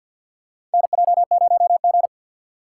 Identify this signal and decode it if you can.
Morse code, heard as a keyed tone, about 52 wpm, 700 Hz. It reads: NJ0G
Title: USA Call Signs - 4x Character Spacing - 50wpm